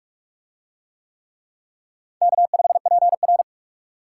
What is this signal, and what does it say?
Morse code, heard as a keyed tone, about 44 wpm, 700 Hz. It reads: K5PR